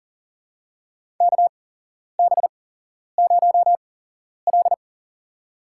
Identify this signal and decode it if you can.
Morse code, heard as a keyed tone, about 37 wpm, spaced as for 11 wpm, 700 Hz. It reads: KB0L
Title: USA Call Signs - 4 Characters - 8x Character Spacing - 40wpm